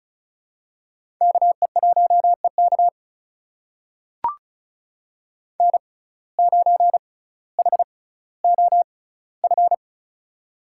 Morse code, heard as a keyed tone, about 36 wpm, spaced as for 18 wpm, 700 Hz. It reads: KE1EK N9HOF
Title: USA Call Signs - 5 Characters - 6x Character Spacing - 35wpm